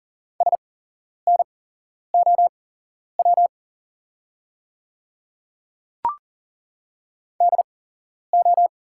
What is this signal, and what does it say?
Morse code, heard as a keyed tone, about 40 wpm, spaced as for 11 wpm, 690 Hz. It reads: SNOW DO
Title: 8x Character Spacing - Top 500 Words - 40wpm